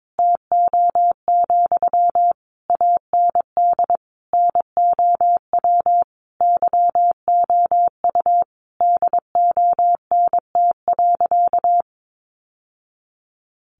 Morse code, heard as a keyed tone, about 22 wpm, 700 Hz. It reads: TO, AND NOW YOU DONT.
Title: Sentences from Top 600 Words 22wpm